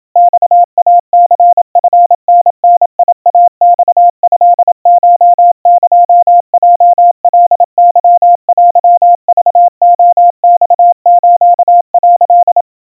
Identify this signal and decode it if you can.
Morse code (Japanese Wabun code), heard as a keyed tone, about 27 wpm, 700 Hz. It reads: マイニチタダイマトコエヲカケテクレマス」
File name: kawa / Morse sample / kamyu 100 cpm 700 Hz Crispy